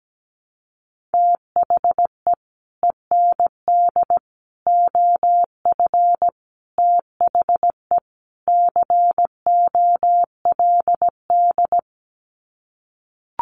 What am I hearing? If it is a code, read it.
Morse code, heard as a keyed tone, about 17 wpm, 700 Hz. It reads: THE END OF THE COLD
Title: Sets of 5 Words 17wpm